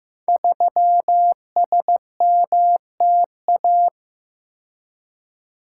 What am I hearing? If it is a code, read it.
Morse code, heard as a keyed tone, about 15 wpm, 700 Hz. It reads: 3SMTA
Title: Code Groups - 5 Letter:Numbers - 15wpm